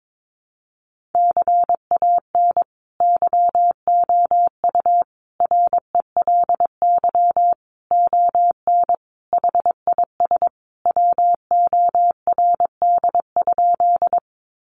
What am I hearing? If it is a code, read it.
Morse code, heard as a keyed tone, about 22 wpm, 700 Hz. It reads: CAN YOU RELY ON HIS WORD?
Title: Sentences from Top 1200 Words 22wpm